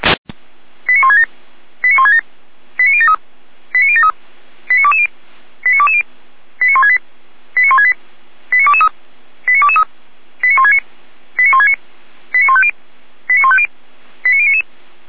Five-tone selective call (ZVEI), heard as quick runs of five tones, tones of about 70 ms. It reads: 89167 89167 89073 89073 89220 89220 89168 89168 89223 89223 89169 89169 89150 89150 89099
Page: SDR/ZVEI